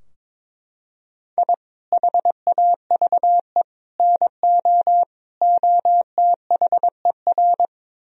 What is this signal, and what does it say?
Morse code, heard as a keyed tone, about 21 wpm, 700 Hz. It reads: I HAVE NO OTHER
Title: Sentences from Top 1000 Words - Review Part 3 - 22wpm